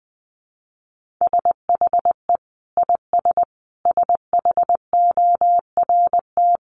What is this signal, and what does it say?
Morse code, heard as a keyed tone, about 20 wpm, 700 Hz. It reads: SHE IS SHORT
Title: Sentences from Top 1200 Words 20wpm